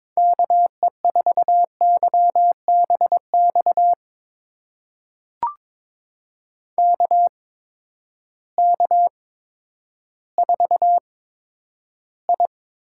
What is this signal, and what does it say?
Morse code, heard as a keyed tone, about 22 wpm, spaced as for 11 wpm, 700 Hz. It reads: KE4YBX KK4I
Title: USA Call Signs - 8x Character Spacing - 22wpm